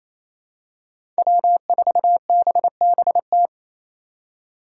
Morse code, heard as a keyed tone, about 28 wpm, 700 Hz. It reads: W4BBT
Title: K1USN SST QSO Practice - 28wpm